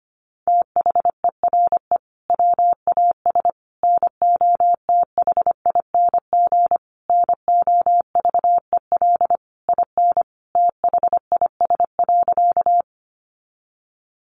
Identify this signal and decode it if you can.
Morse code, heard as a keyed tone, about 25 wpm, 700 Hz. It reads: THERE WAS NOTHING NOVEL IN THIS.
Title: Sentences from Top 1300 Words 25wpm